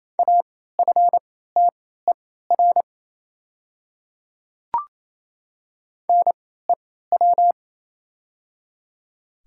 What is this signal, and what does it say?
Morse code, heard as a keyed tone, about 28 wpm, spaced as for 16 wpm, 700 Hz. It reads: AFTER NEW
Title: 3x Character Spacing - Top 200 Words - 28wpm